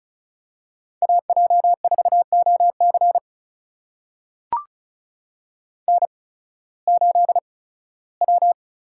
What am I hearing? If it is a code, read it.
Morse code, heard as a keyed tone, about 36 wpm, spaced as for 19 wpm, 700 Hz. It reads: AJ4OC N8W
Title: USA Call Signs - 5 Characters - 8x Character Spacing - 35wpm